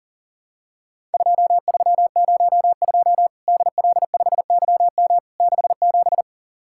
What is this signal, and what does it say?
Morse code, heard as a keyed tone, about 43 wpm, 700 Hz. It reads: 2302 DL5YM 67
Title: Worked All Europe DX Contest - QTC Exchange - 40wpm